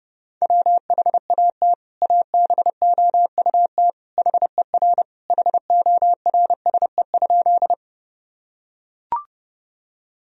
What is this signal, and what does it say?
Morse code, heard as a keyed tone, about 30 wpm, 700 Hz. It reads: WHAT ABOUT HER HORSE?
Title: Sentences from Top 1000 Words - Review Part 1 - 30wpm